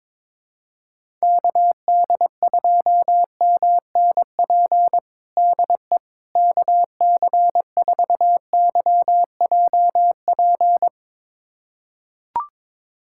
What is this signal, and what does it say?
Morse code, heard as a keyed tone, about 22 wpm, 700 Hz. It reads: KD2MNP DE KC4YJP
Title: USA Callsign DE Callsign 22wpm